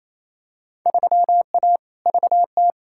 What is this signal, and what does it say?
Morse code, heard as a keyed tone, about 28 wpm, 700 Hz. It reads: 3A VT
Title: Field Day Exchanges Encore - 28wpm